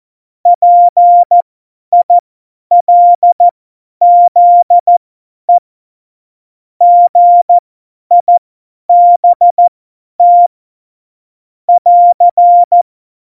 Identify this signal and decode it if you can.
Morse code, heard as a keyed tone, about 14 wpm, spaced as for 10 wpm, 700 Hz. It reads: PILZE GIBT <AR>
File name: Morsekurs / Witze / Witz 21. Mai 2019, 50 F70